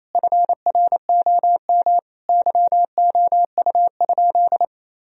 Morse code, heard as a keyed tone, about 28 wpm, 700 Hz. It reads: FROM YOU?
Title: Sentences from Top 1000 Words - Review Part 1 - 28wpm